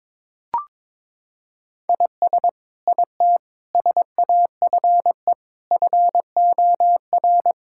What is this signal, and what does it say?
Morse code, heard as a keyed tone, about 22 wpm, 700 Hz. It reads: IS IT SAFE FOR
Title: Sentences from Top 1600 Words 22wpm